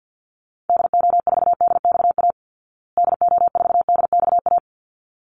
Morse code, heard as a keyed tone, about 50 wpm, 700 Hz. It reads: DO4DXA DO4DXA
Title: Call Signs Repeated 3x - 50wpm